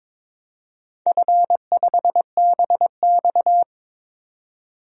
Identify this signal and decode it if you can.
Morse code, heard as a keyed tone, about 22 wpm, 700 Hz. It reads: F5BX
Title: DXCC-100 Call Signs - 4 Characters - 22wpm